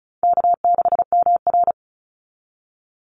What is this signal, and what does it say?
Morse code, heard as a keyed tone, about 34 wpm, 700 Hz. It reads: K6MR
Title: Call Signs - 35wpm